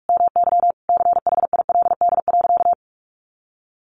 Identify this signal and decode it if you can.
Morse code, heard as a keyed tone, about 44 wpm, 700 Hz. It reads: MY CHILD.
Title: Sentences from Top 1400 Words 45wpm